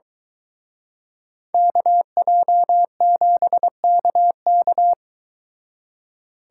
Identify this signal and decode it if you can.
Morse code, heard as a keyed tone, about 23 wpm, 700 Hz. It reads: KJ7KK